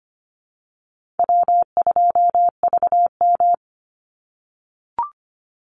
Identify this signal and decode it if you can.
Morse code, heard as a keyed tone, about 25 wpm, 700 Hz. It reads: W2VM
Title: Easy Call Signs - 25wpm